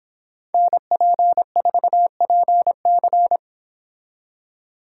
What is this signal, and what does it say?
Morse code, heard as a keyed tone, about 31 wpm, 700 Hz. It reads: NP4PC